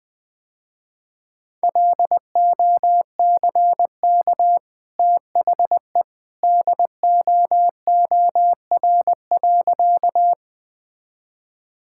Morse code, heard as a keyed tone, about 20 wpm, 700 Hz. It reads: LOCK THE DOOR.
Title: Easy Sentences with Hints - 20wpm